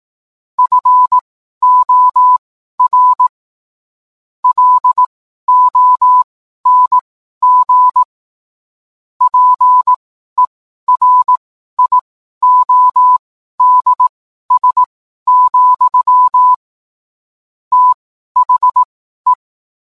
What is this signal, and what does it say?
Morse code, heard as a keyed tone, about 18 wpm, spaced as for 12 wpm, 1000 Hz. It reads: FOR LONG PERIODS, THE
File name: cwblog / cw mp3 directory / mem morse 12